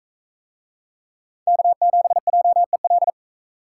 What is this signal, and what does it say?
Morse code, heard as a keyed tone, about 41 wpm, 700 Hz. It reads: K7JEL